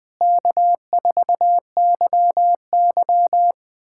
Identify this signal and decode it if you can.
Morse code, heard as a keyed tone, about 20 wpm, 700 Hz. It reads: K4YY
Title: USA Callsign DE Callsign - 4 Characters - 20wpm